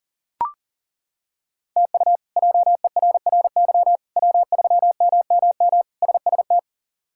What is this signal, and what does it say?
Morse code, heard as a keyed tone, about 37 wpm, 700 Hz. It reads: TU JERRY W3MMM SST
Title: K1USN SST QSO Practice - 40wpm